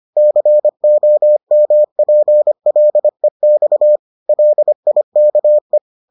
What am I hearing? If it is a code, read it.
Morse code, heard as a keyed tone, about 25 wpm, 600 Hz. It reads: COMPLEX LIKE